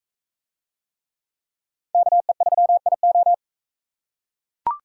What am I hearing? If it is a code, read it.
Morse code, heard as a keyed tone, about 38 wpm, 700 Hz. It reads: KE3IO